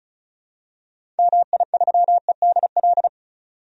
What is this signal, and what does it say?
Morse code, heard as a keyed tone, about 34 wpm, 700 Hz. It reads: MI3EDL
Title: DXCC-100 Call Signs - 6 Characters - 35wpm